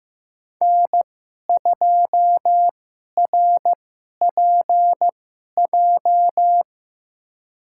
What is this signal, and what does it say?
Morse code, heard as a keyed tone, about 15 wpm, spaced as for 11 wpm, 700 Hz. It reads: N2RPJ
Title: USA Call Signs - 5 Characters - 2x Character Spacing - 15wpm